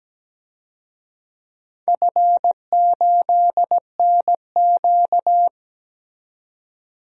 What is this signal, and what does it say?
Morse code, heard as a keyed tone, about 17 wpm, 700 Hz. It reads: F8NQ
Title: DXCC-100 Call Signs - 4 Characters - 17wpm